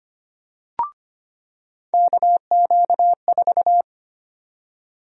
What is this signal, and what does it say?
Morse code, heard as a keyed tone, about 25 wpm, 700 Hz. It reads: KQ4
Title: USA Call Sign Prefixes - 25wpm